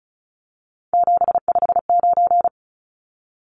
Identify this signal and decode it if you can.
Morse code, heard as a keyed tone, about 34 wpm, 700 Hz. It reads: 759
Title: Numbers - 3-Digit 35wpm